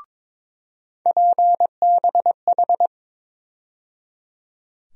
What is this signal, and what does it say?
Morse code, heard as a keyed tone, about 21 wpm, 700 Hz. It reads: PBH
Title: USA Call Sign Postfixes - 3 Letters 22wpm